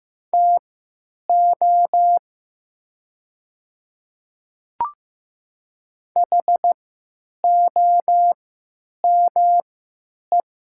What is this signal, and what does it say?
Morse code, heard as a keyed tone, about 15 wpm, spaced as for 9 wpm, 700 Hz. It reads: TO HOME
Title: 3x Character Spacing - Top 500 Words - 15wpm